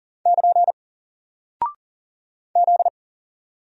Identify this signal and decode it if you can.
Morse code, heard as a keyed tone, about 40 wpm, 700 Hz. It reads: <KN> Z
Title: Single Character (Letter:Number:Punctuation:Prosign) 40wpm